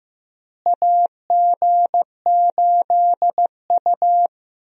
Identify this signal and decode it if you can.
Morse code, heard as a keyed tone, about 15 wpm, 700 Hz. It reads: AG8U
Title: USA Call Signs - 4 Characters - Repeated 3x - 15wpm